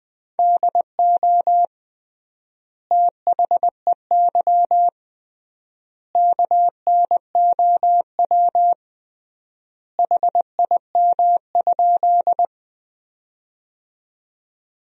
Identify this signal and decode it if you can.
Morse code, heard as a keyed tone, about 20 wpm, 700 Hz. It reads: DO THEY KNOW HIM?
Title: 3x Word-Spacing - Sentences from Top 100 Words - 20wpm